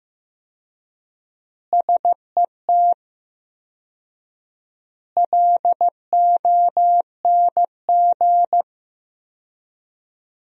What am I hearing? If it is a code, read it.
Morse code, heard as a keyed tone, about 15 wpm, 700 Hz. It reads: SET LONG
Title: 4x Word-Spacing - Sentences from Top 300 Words - 15wpm